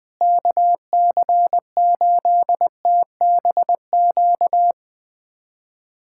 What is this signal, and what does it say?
Morse code, heard as a keyed tone, about 21 wpm, 700 Hz. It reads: KC8TBQ
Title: USA Callsign DE Callsign 20wpm